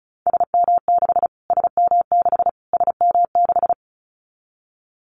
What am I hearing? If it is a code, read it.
Morse code, heard as a keyed tone, about 34 wpm, 700 Hz. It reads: SM6 SM6 SM6
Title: Callsign Prefixes for Top 25 DXCC - Repeated 3x - 35wpm